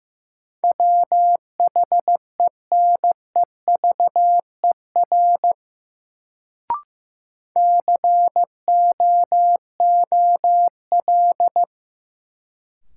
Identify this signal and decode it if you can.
Morse code, heard as a keyed tone, about 16 wpm, 700 Hz. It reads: WHENEVER COOL
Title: Sentences from Top 1200 Words - New Word Drill Alt - 15wpm